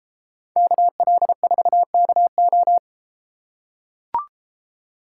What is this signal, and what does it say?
Morse code, heard as a keyed tone, about 33 wpm, 700 Hz. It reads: KL4KO